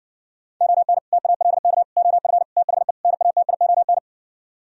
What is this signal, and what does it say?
Morse code, heard as a keyed tone, about 66 wpm, 700 Hz. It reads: ON TALK OF THE NATION